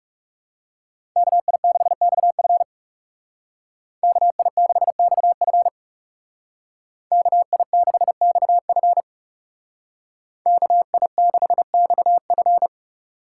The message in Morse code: KI6XF KI6XF KI6XF KI6XF

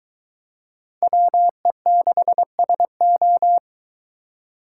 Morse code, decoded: WE6SO

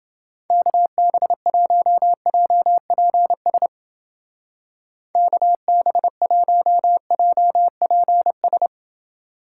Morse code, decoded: KB1JPS KB1JPS

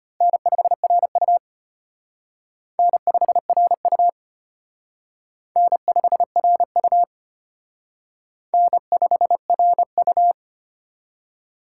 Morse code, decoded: N5RU N5RU N5RU N5RU

